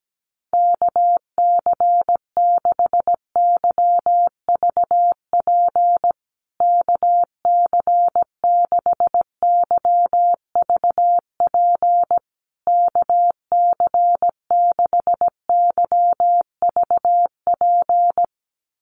Morse code, heard as a keyed tone, about 17 wpm, 700 Hz. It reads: KC6YVP KC6YVP KC6YVP
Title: USA Call Signs Repeated 3x 17wpm